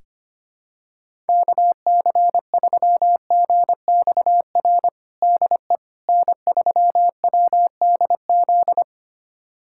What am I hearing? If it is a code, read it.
Morse code, heard as a keyed tone, about 25 wpm, 700 Hz. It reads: KC3GXR DE N3WDZ
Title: USA Callsign DE Callsign 25wpm